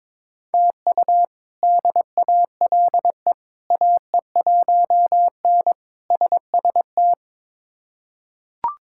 Morse code, heard as a keyed tone, about 21 wpm, 700 Hz. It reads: TU DALE AE1N SST
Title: K1USN SST QSO Practice - 22wpm